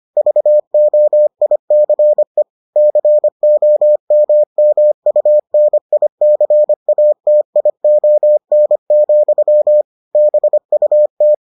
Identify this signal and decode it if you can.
Morse code, heard as a keyed tone, about 25 wpm, 600 Hz. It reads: VOICE COMMUNICATION, BUT